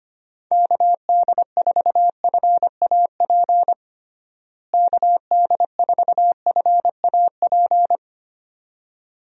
Morse code, heard as a keyed tone, about 25 wpm, 700 Hz. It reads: KD4FAP KD4FAP
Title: USA Call Signs - 6 Characters - Repeated 2x - 25wpm